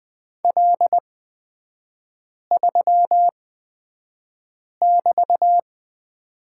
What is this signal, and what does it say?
Morse code, heard as a keyed tone, about 20 wpm, spaced as for 5 wpm, 700 Hz. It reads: L3<BT>
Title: 224 Lesson 36e - BT - Three Characters 20wpm